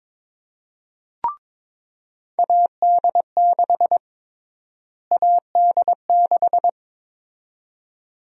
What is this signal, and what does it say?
Morse code, heard as a keyed tone, about 22 wpm, 700 Hz. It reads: AD6 AD6
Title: USA Call Sign Prefixes - Repeated 2x - 22wpm